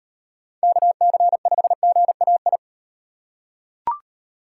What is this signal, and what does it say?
Morse code, heard as a keyed tone, about 36 wpm, 700 Hz. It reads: KC5GAI